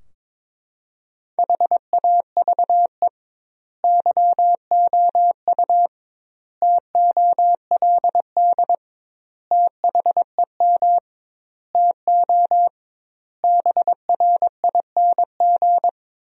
Morse code, heard as a keyed tone, about 22 wpm, 700 Hz. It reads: HAVE YOU TOLD THEM TO BRING
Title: 2x Word-Spacing - Sentences from Top 500 Words - 22wpm